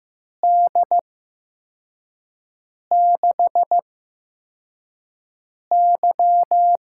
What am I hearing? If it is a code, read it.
Morse code, heard as a keyed tone, about 15 wpm, spaced as for 4 wpm, 700 Hz. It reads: D6Y
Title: USA Call Signs - 6 Characters - 8x Character Spacing - 15wpm